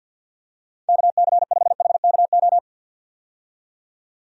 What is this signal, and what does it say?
Morse code, heard as a keyed tone, about 46 wpm, 700 Hz. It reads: KC5HKO
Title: USA Call Signs - 6 Characters - 50wpm